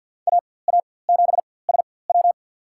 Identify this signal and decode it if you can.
Morse code, heard as a keyed tone, about 52 wpm, spaced as for 23 wpm, 700 Hz. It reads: AA7SW
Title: USA Call Signs - 4x Character Spacing - 50wpm